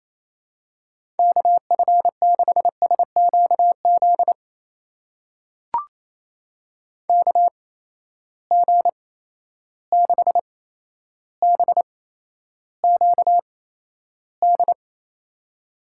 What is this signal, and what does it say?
Morse code, heard as a keyed tone, about 28 wpm, spaced as for 12 wpm, 700 Hz. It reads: KF6SQZ KG6BQD
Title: USA Call Signs - 6 Characters - 8x Character Spacing - 28wpm